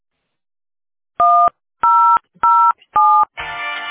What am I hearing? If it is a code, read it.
DTMF, heard as telephone touch tones, tones of about 300 ms. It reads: 1007